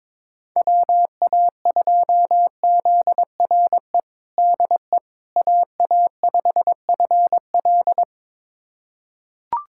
Morse code, heard as a keyed tone, about 22 wpm, 700 Hz. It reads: WA2ZRE DE AA5FL
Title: USA Callsign DE Callsign 22wpm